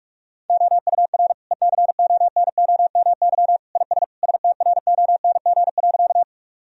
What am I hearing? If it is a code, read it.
Morse code, heard as a keyed tone, about 46 wpm, 700 Hz. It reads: OUR ECONOMY IS STRONG.
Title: Sentences from Top 1000 Words - Review Part 3 - 45wpm